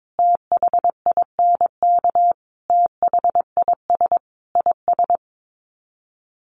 Morse code, heard as a keyed tone, about 22 wpm, 700 Hz. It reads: THINK THIS IS
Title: Sets of 5 Words Encore 22wpm